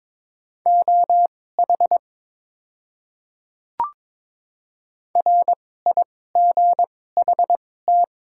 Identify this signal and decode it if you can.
Morse code, heard as a keyed tone, about 22 wpm, spaced as for 16 wpm, 700 Hz. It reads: OH RIGHT
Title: 2x Character Spacing - Top 500 Words - 22wpm